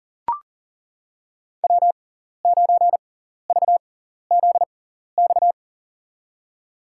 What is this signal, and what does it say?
Morse code, heard as a keyed tone, about 37 wpm, spaced as for 14 wpm, 700 Hz. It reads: W9VZX
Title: USA Call Signs - 5 Characters - 6x Character Spacing - 40wpm